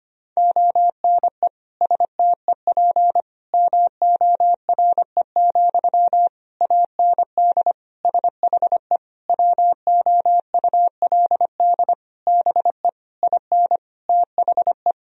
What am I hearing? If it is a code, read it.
Morse code, heard as a keyed tone, about 25 wpm, 700 Hz. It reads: ONE STEP MORE, AND SHE WOULD BE IN THE
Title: Sentences from Top 1000 Words - Review Part 3 - 25wpm